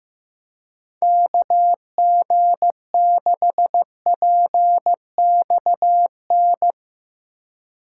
Morse code, heard as a keyed tone, about 16 wpm, 700 Hz. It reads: KG6PXN
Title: Speed Racing - USA 2x3 Call Signs 15wpm